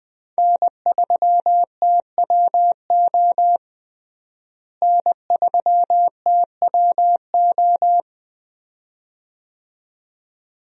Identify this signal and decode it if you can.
Morse code, heard as a keyed tone, about 20 wpm, 700 Hz. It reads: N3TWO N3TWO
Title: USA Call Signs - 5 Characters - Repeated 2x - 20wpm